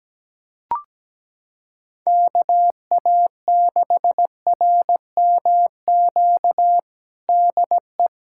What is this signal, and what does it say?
Morse code, heard as a keyed tone, about 17 wpm, 700 Hz. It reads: KA6RMQ DE
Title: USA Callsign DE Callsign - 6 Characters - 17wpm